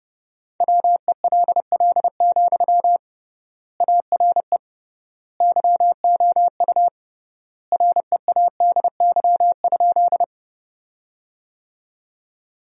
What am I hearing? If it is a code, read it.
Morse code, heard as a keyed tone, about 30 wpm, 700 Hz. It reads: WELL, ARE YOU READY?
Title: 3x Word-Spacing - Sentences from Top 400 Words - 30wpm